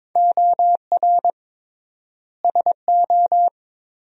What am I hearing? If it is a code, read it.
Morse code, heard as a keyed tone, about 21 wpm, 700 Hz. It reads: OR SO
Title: 3x Word-Spacing - Sentences from Top 100 Words - 22wpm